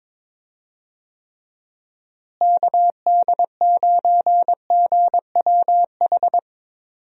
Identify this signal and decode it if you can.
Morse code, heard as a keyed tone, about 22 wpm, 700 Hz. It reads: KD9GWH
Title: USA Call Signs - 6 Characters - Repeated 2x - 22wpm